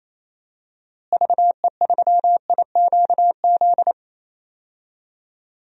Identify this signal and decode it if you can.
Morse code, heard as a keyed tone, about 27 wpm, 700 Hz. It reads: VE3IQZ
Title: DXCC-100 Call Signs - 6 Characters - Repeated 3x - 28wpm